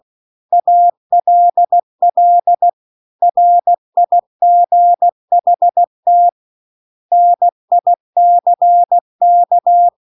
Morse code, heard as a keyed tone, about 16 wpm, 700 Hz. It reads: ALL RIGHT NICK